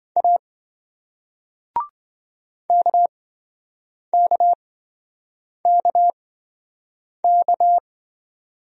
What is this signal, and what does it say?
Morse code, heard as a keyed tone, about 28 wpm, spaced as for 7 wpm, 700 Hz. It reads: A KKKK